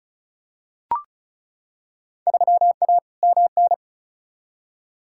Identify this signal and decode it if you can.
Morse code, heard as a keyed tone, about 34 wpm, 700 Hz. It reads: 3A MN
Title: Field Day Exchanges - 35wpm